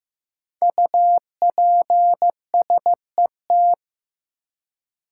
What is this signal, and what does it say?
Morse code, heard as a keyed tone, about 15 wpm, 700 Hz. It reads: UPSET
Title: Five Letter Words - 15wpm